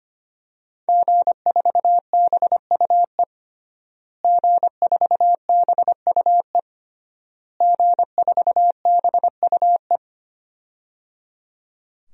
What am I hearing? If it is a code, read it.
Morse code, heard as a keyed tone, about 25 wpm, 700 Hz. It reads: G4BUE G4BUE G4BUE
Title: DXCC-100 Call Signs - 5 Characters - Repeated 3x - 25wpm